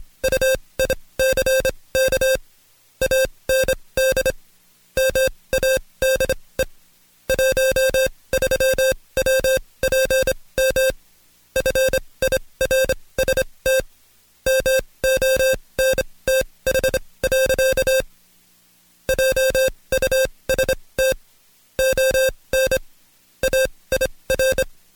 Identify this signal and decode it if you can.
Morse code, heard as a keyed tone, about 26 wpm, spaced as for 20 wpm, 530 Hz. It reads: UICK AND MADE 13WPM FIRST MONTH. JUST ON AIR
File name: Code Quick / AudioDownld2 / Seg136-Stepper9 19WPM